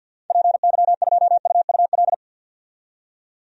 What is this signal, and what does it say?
Morse code, heard as a keyed tone, about 50 wpm, 700 Hz. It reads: PY2UUL
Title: DXCC-100 Call Signs - 6 Characters - 50wpm